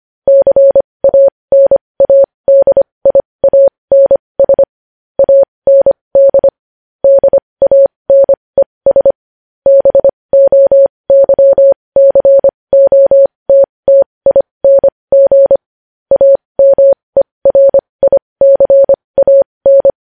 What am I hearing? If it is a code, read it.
Morse code, heard as a keyed tone, about 25 wpm, spaced as for 20 wpm, 550 Hz. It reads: CANADIANS AND DANES BOYCOTTING AMERICAN
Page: news headlines in morse code at 20 wpm